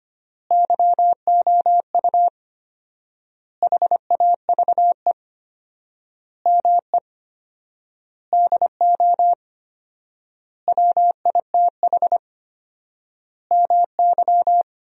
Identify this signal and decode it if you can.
Morse code, heard as a keyed tone, about 25 wpm, 700 Hz. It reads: YOU HAVE ME DO WITH MY
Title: 4x Word-Spacing - Sentences from Top 300 Words - 25wpm